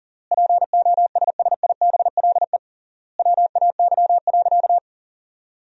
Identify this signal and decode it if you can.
Morse code, heard as a keyed tone, about 37 wpm, 700 Hz. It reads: POSSIBLE WAY.
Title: 3x Word-Spacing - Sentences from Top 500 Words - 40wpm